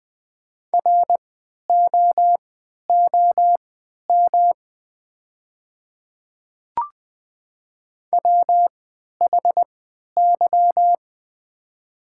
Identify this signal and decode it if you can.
Morse code, heard as a keyed tone, about 20 wpm, spaced as for 11 wpm, 700 Hz. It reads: ROOM WHY
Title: 3x Character Spacing - Top 300 Words - 20wpm